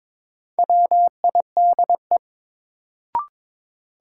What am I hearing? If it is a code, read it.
Morse code, heard as a keyed tone, about 22 wpm, 700 Hz. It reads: WIDE
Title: Sentences from Top 900 Words - New Word Drill - 22wpm